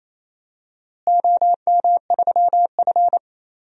Morse code, heard as a keyed tone, about 28 wpm, 700 Hz. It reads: OM3F
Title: DXCC-100 Call Signs - 4 Characters - Repeated 2x - 28wpm